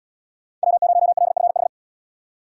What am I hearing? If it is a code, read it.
Morse code, heard as a keyed tone, about 72 wpm, 700 Hz. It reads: W0LFH